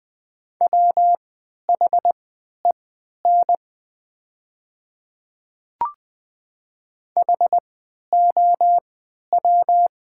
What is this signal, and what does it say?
Morse code, heard as a keyed tone, about 20 wpm, spaced as for 11 wpm, 700 Hz. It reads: WHEN HOW
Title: 3x Character Spacing - Top 100 Words - 20wpm